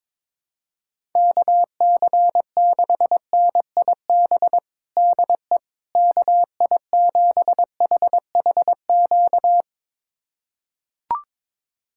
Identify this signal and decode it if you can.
Morse code, heard as a keyed tone, about 21 wpm, 700 Hz. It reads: KC6NIB DE KI7HHQ
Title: USA Callsign DE Callsign - 6 Characters - 22wpm